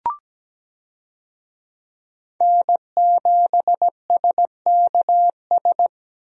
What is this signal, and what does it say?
Morse code, heard as a keyed tone, about 17 wpm, 700 Hz. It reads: N7SKS
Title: USA Call Signs - 5 Characters - Repeated 2x - 17wpm